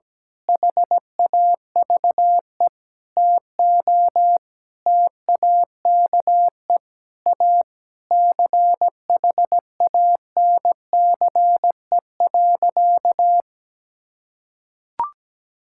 Morse code, heard as a keyed tone, about 17 wpm, 700 Hz. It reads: HAVE TO TAKE A CHANCE.
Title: Sentences from Top 1000 Words - Review Part 6 - 17wpm